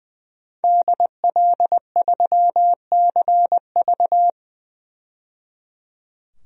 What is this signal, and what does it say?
Morse code, heard as a keyed tone, about 21 wpm, 700 Hz. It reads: DL3CV
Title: DXCC-100 Call Signs - 5 Characters - 20wpm